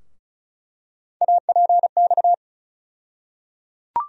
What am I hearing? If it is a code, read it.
Morse code, heard as a keyed tone, about 34 wpm, 700 Hz. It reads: APX
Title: USA Call Sign Postfixes - 3 Letters 35wpm